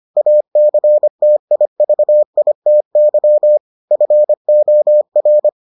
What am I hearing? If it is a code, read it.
Morse code, heard as a keyed tone, about 25 wpm, 600 Hz. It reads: ACTIVITY FOR